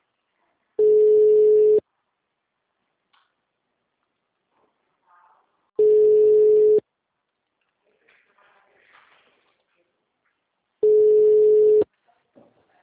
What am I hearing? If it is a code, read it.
Ringback tone (Europe, 425 Hz).